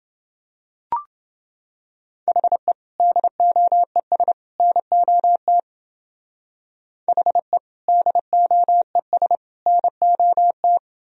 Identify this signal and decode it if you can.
Morse code, heard as a keyed tone, about 28 wpm, 700 Hz. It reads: HE DOES NOT HE DOES NOT